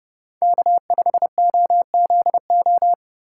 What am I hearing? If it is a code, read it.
Morse code, heard as a keyed tone, about 30 wpm, 700 Hz. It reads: K5OZO